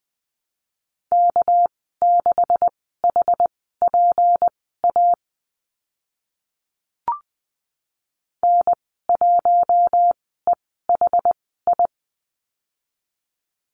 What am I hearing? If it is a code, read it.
Morse code, heard as a keyed tone, about 20 wpm, spaced as for 14 wpm, 700 Hz. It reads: K6HPA N1EHI
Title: 213 Lesson 34g - Number 8 - Call Signs 20wpm